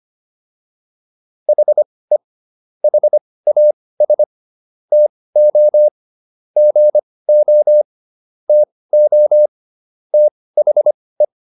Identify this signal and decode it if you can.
Morse code, heard as a keyed tone, about 25 wpm, spaced as for 18 wpm, 600 Hz. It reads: HE HAS TO GO TO THE